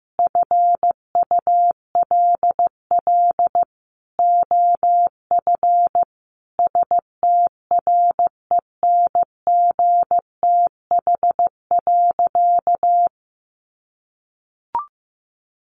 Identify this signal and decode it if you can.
Morse code, heard as a keyed tone, about 15 wpm, 700 Hz. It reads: FULL OF STRENGTH.
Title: Sentences from Top 1300 Words 15wpm